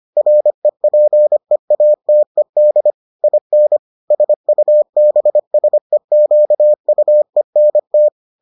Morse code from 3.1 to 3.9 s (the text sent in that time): IN